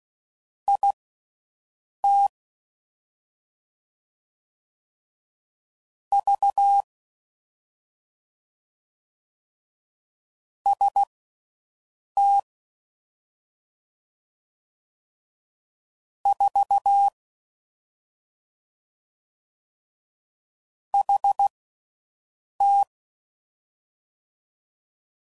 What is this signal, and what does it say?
Morse code, heard as a keyed tone, about 16 wpm, spaced as for 6 wpm, 800 Hz. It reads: IT V ST 4 HT
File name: Mixed / CW Comparisons / Comparisons Broken Characters